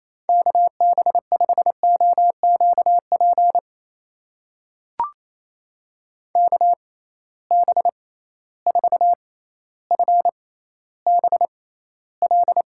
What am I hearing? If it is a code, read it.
Morse code, heard as a keyed tone, about 28 wpm, spaced as for 14 wpm, 700 Hz. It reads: KB5OQP KB4FBL
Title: USA Call Signs - 6 Characters - 6x Character Spacing - 28wpm